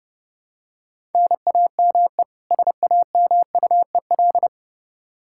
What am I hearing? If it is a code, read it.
Morse code, heard as a keyed tone, about 32 wpm, 700 Hz. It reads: NAME SAMUEL
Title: Practice QSO - Exchange 1 - Beginning - 30wpm